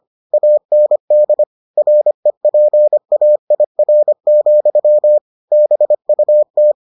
Morse code, heard as a keyed tone, about 25 wpm, 600 Hz. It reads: AND REPAIR, BUT